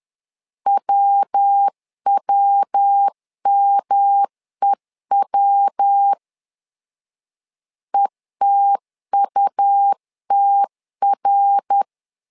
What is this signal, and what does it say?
Morse code, heard as a keyed tone, about 10 wpm, 800 Hz. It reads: WWMEW ETUTR
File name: LEZIONI / 3 MWRO / 3 MWR 10wpm 1952